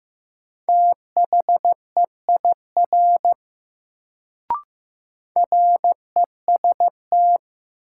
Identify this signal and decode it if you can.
Morse code, heard as a keyed tone, about 15 wpm, 700 Hz. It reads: THEIR REST
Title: Sets of 2 Words - New Word Drill - 15wpm